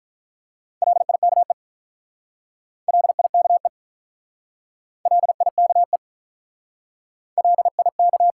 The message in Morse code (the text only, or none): LIKE LIKE LIKE LIK